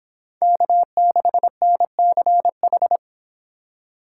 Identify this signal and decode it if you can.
Morse code, heard as a keyed tone, about 29 wpm, 700 Hz. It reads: K6NCH